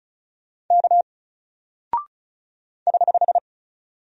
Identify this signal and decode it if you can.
Morse code, heard as a keyed tone, about 34 wpm, 700 Hz. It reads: K <HH>
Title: Single Character (Letter:Number:Punctuation:Prosign) 35wpm